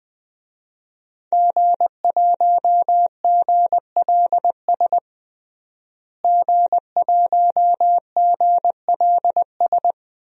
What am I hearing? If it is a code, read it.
Morse code, heard as a keyed tone, about 20 wpm, 700 Hz. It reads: G1GLS G1GLS
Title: DXCC-100 Call Signs - 5 Characters - Repeated 2x - 20wpm